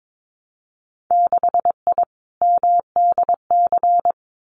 Morse code, heard as a keyed tone, about 22 wpm, 700 Hz. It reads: 6I MDC
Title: Winter Field Day Exchanges Encore 22wpm